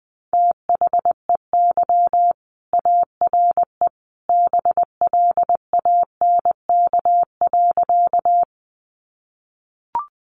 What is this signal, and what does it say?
Morse code, heard as a keyed tone, about 20 wpm, 700 Hz. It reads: THEY ARE BLANK.
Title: Sentences from Top 1500 Words 20wpm